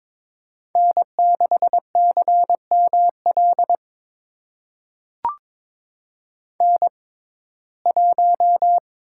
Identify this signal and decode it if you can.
Morse code, heard as a keyed tone, about 22 wpm, spaced as for 16 wpm, 700 Hz. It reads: N6CML N1
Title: USA Call Signs - 5 Characters - 6x Character Spacing - 22wpm